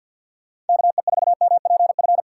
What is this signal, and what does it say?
Morse code, heard as a keyed tone, about 50 wpm, 700 Hz. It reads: KE4MPF
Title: USA Call Signs - 6 Characters - 4x Character Spacing - 50wpm